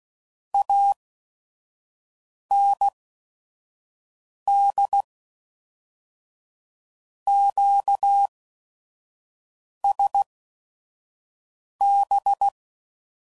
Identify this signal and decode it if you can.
Morse code, heard as a keyed tone, about 16 wpm, spaced as for 5 wpm, 800 Hz. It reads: AND QSB
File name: CW Code Groups @5WPM / QSOs / WD0FBT 1